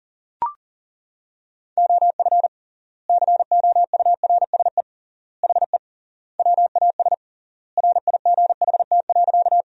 Morse code, heard as a keyed tone, about 40 wpm, 700 Hz. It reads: OF COURSE HE WAS RIGHT.
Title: 3x Word-Spacing - Sentences from Top 500 Words - 40wpm